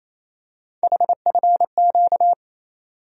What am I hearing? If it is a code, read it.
Morse code, heard as a keyed tone, about 28 wpm, 700 Hz. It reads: HFQ
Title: USA Call Sign Postfixes - 3 Letters 28wpm